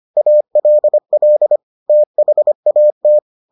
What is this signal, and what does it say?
Morse code, heard as a keyed tone, about 25 wpm, 600 Hz. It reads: ALL THAT